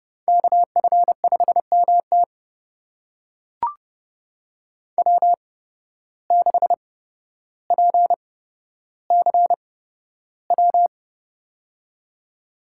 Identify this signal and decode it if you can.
Morse code, heard as a keyed tone, about 30 wpm, spaced as for 13 wpm, 700 Hz. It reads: KF5MT W6PCW
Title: USA Call Signs - 5 Characters - 8x Character Spacing - 30wpm